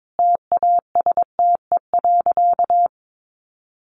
Morse code, heard as a keyed tone, about 22 wpm, 700 Hz. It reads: TASTE.
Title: Sentences from Top 1100 Words 22wpm